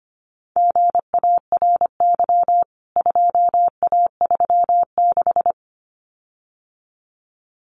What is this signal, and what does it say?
Morse code, heard as a keyed tone, about 25 wpm, 700 Hz. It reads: GARY 2A36
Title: CWT Exchange 25wpm